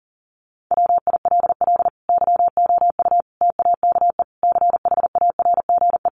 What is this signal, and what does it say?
Morse code, heard as a keyed tone, about 40 wpm, 700 Hz. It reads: WILL YOU TAKE CHARGE
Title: Sentences from Top 900 Words 40wpm